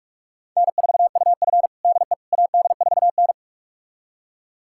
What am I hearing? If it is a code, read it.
Morse code, heard as a keyed tone, about 47 wpm, 700 Hz. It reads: N4UF DE AD4N
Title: USA Callsign DE Callsign - 4 Characters - 45wpm